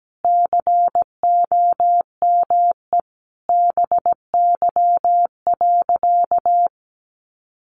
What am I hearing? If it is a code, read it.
Morse code, heard as a keyed tone, about 17 wpm, 700 Hz. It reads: COME BY.
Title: Sentences from Top 1500 Words 17wpm